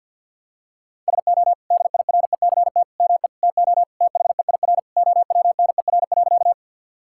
Morse code, heard as a keyed tone, about 50 wpm, 700 Hz. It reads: SO DIRECT ME TO THEIR OWNER.
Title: Sentences from Top 1600 Words 50wpm